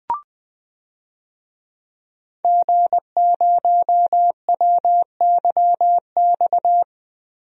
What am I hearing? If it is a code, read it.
Morse code, heard as a keyed tone, about 20 wpm, 700 Hz. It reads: G0WYX
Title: DXCC-100 Call Signs - 5 Characters - Repeated 2x - 20wpm